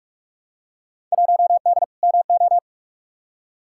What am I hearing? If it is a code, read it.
Morse code, heard as a keyed tone, about 47 wpm, 700 Hz. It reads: 1D MO
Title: Field Day Exchanges - 45wpm